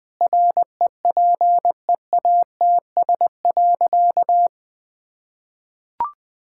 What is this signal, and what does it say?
Morse code, heard as a keyed tone, about 20 wpm, 700 Hz. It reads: REPEATS.
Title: Sentences with 6-Word Phrases - Spoken Beginning - 20wpm